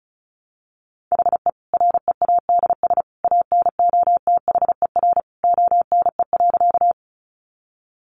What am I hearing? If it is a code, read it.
Morse code, heard as a keyed tone, about 34 wpm, 700 Hz. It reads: HE READS ANOTHER ONE.
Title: Additional Easy Phrases 35wpm